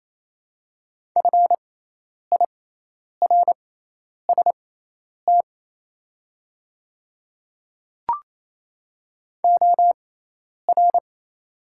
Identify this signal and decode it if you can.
Morse code, heard as a keyed tone, about 28 wpm, spaced as for 10 wpm, 700 Hz. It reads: FIRST OR
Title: 6x Character Spacing - Top 100 Words - 28wpm